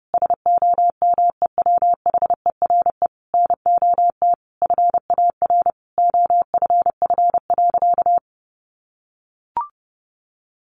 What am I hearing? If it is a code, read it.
Morse code, heard as a keyed tone, about 30 wpm, 700 Hz. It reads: SOMEWHERE NOT FAR OFF.
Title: Sentences from Top 1400 Words 30wpm